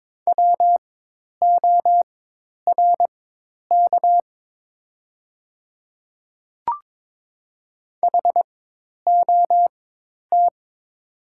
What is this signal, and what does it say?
Morse code, heard as a keyed tone, about 22 wpm, spaced as for 10 wpm, 700 Hz. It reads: WORK HOT
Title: 4x Character Spacing - Top 200 Words - 22wpm